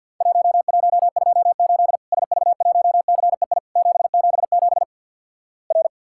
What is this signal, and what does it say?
Morse code, heard as a keyed tone, about 50 wpm, 700 Hz. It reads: 1128 SV1CEI 777 R
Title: Worked All Europe DX Contest - QTC Exchange with Repeats - 50wpm